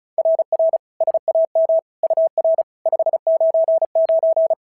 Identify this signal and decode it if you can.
Morse code, heard as a keyed tone, about 33 wpm, 650 Hz. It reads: RR SAM UR 599
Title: SOTA QSO Practice 35wpm